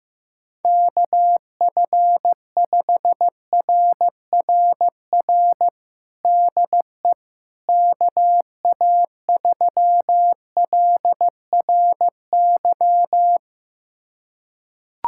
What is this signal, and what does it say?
Morse code, heard as a keyed tone, about 15 wpm, 700 Hz. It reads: KF5RRR DE KA3LRY
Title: USA Callsign DE Callsign - 6 Characters - 15wpm